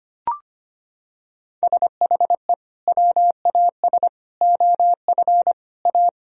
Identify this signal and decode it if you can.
Morse code, heard as a keyed tone, about 25 wpm, 700 Hz. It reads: SHE WAS OF A
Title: Sentences from Top 1000 Words - Review Part 2 - 25wpm